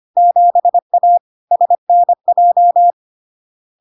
Morse code, heard as a keyed tone, about 25 wpm, 700 Hz. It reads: 7A SNJ
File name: Log-in-CW-25wpm-700-Hz